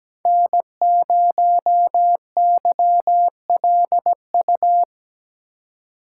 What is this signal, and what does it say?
Morse code, heard as a keyed tone, about 17 wpm, 700 Hz. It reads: N0YLU
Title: USA Call Signs - 5 Characters - 3x Character Spacing - 17wpm